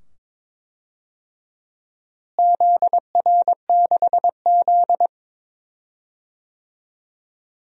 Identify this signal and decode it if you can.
Morse code, heard as a keyed tone, about 21 wpm, 700 Hz. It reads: ZR6Z
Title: DXCC-100 Call Signs - 4 Characters - Repeated 2x - 22wpm